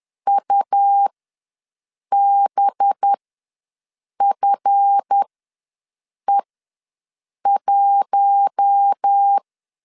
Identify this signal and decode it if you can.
Morse code, heard as a keyed tone, about 10 wpm, spaced as for 6 wpm, 800 Hz. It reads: UBFE1
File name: LEZIONI / 10 SIGNS / 10D COMA ws 10wpm 4314